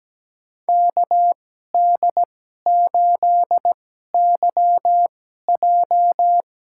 Morse code, heard as a keyed tone, about 17 wpm, spaced as for 12 wpm, 700 Hz. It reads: KD8YJ
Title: USA Call Signs - 2x Character Spacing - 17wpm